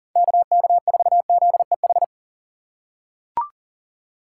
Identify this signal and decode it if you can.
Morse code, heard as a keyed tone, about 40 wpm, 700 Hz. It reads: KK4ZEH
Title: USA Call Signs - 6 Characters - 4x Character Spacing - 40wpm